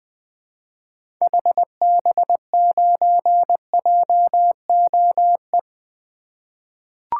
Morse code, heard as a keyed tone, about 21 wpm, 700 Hz. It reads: HB9JOE
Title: 2023 SOTA Activator Call Signs - 20wpm